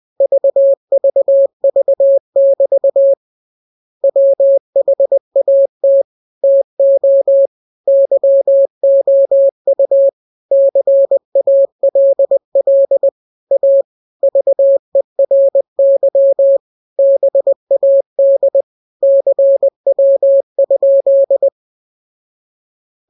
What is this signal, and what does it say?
Morse code, heard as a keyed tone, about 20 wpm, 550 Hz. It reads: VVV<BT> WHAT TO YOU CALL A VERY BAD CW?